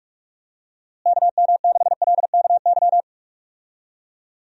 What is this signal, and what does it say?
Morse code, heard as a keyed tone, about 46 wpm, 700 Hz. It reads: KM6LKY
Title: USA Call Signs - 6 Characters - Repeated 2x - 45wpm